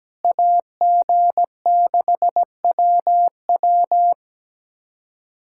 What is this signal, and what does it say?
Morse code, heard as a keyed tone, about 17 wpm, 700 Hz. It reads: AG6WW
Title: USA Callsign DE Callsign - 5 Characters - 17wpm